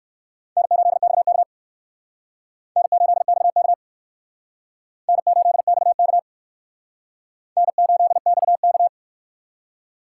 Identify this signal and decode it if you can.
Morse code, heard as a keyed tone, about 56 wpm, 700 Hz. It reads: N8XK N8XK N8XK N8XK
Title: Speed Racing - USA 1x2 Call Signs 45wpm